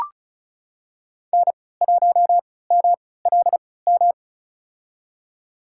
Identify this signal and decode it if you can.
Morse code, heard as a keyed tone, about 34 wpm, spaced as for 20 wpm, 700 Hz. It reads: N1MLM
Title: USA Call Signs - 3x Character Spacing - 35wpm